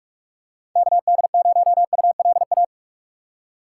Morse code, heard as a keyed tone, about 47 wpm, 690 Hz. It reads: KD0ULA
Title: USA Call Signs - 6 Characters - 3x Character Spacing - 45wpm